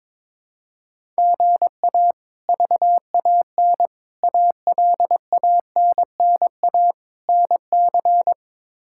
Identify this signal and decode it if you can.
Morse code, heard as a keyed tone, about 22 wpm, 700 Hz. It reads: GA VAN ALANNA NC
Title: K1USN SST QSO Practice - 22wpm